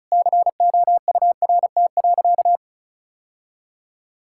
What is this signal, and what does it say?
Morse code, heard as a keyed tone, about 36 wpm, 700 Hz. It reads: COURT.
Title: Sentences from Top 1600 Words 35wpm